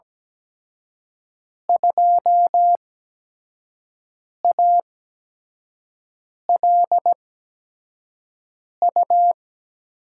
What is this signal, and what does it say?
Morse code, heard as a keyed tone, about 17 wpm, spaced as for 5 wpm, 700 Hz. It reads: 2ALU
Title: USA Call Signs - 5 Characters - 8x Character Spacing - 17wpm